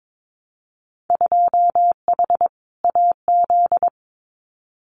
Morse code, heard as a keyed tone, about 22 wpm, 700 Hz. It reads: 2H AZ
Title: Winter Field Day Exchanges 22wpm